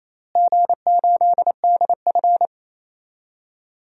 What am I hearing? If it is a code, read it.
Morse code, heard as a keyed tone, about 28 wpm, 700 Hz. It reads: G8DF
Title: DXCC-100 Call Signs - 4 Characters - Repeated 3x - 28wpm